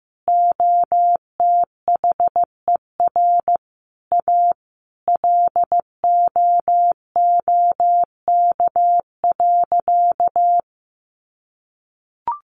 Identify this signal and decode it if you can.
Morse code, heard as a keyed tone, about 15 wpm, 700 Hz. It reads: OTHER A LOOK.
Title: A Sound Of Thunder 15wpm